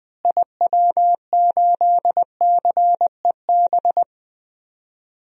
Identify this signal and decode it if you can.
Morse code, heard as a keyed tone, about 20 wpm, 700 Hz. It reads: IW8CEB
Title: DXCC-100 Call Signs - 6 Characters - Repeated 2x - 20wpm